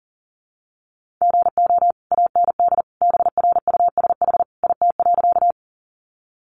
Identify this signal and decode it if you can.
Morse code, heard as a keyed tone, about 37 wpm, 700 Hz. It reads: GO AND BRUSH IT.
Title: Sentences from Top 1100 Words 40wpm